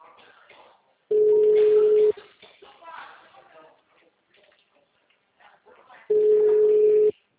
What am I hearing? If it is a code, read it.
Ringback tone (Europe, 425 Hz), with speech.